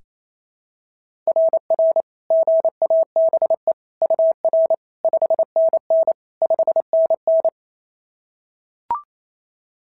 Morse code, heard as a keyed tone, about 25 wpm, 650 Hz. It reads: RR GABE UR 5NN 5NN
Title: SOTA QSO Practice 28wpm